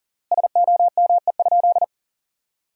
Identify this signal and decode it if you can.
Morse code, heard as a keyed tone, about 37 wpm, 700 Hz. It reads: SOME?
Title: 3x Word-Spacing - Sentences from Top 100 Words - 40wpm